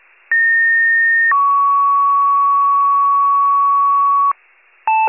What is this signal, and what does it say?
Two-tone page, heard as a one-second tone, then a three-second tone, about 1850 then 1120 Hz, with noise. It is heard from 0.3 to 4.3 s.